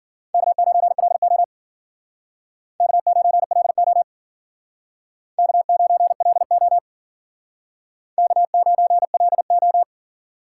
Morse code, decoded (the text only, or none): K9LO K9LO K9LO K9LO